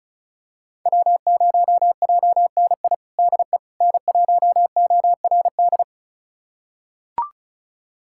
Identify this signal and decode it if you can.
Morse code, heard as a keyed tone, about 34 wpm, 700 Hz. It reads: W0JNI DE N1ORD
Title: USA Callsign DE Callsign - 5 Characters - 35wpm